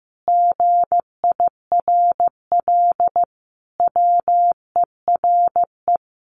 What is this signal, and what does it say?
Morse code, heard as a keyed tone, about 14 wpm, 700 Hz. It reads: GIRL WERE
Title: Sentences from Top 1200 Words 15wpm